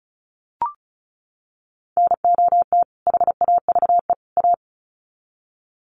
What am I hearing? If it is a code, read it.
Morse code, heard as a keyed tone, about 34 wpm, 700 Hz. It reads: NOT HAVE A
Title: Sets of 3 Words 35wpm